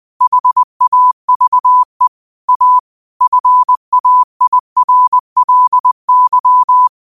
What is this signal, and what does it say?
Morse code, heard as a keyed tone, about 20 wpm, 1000 Hz. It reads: HAVE A FAIRLY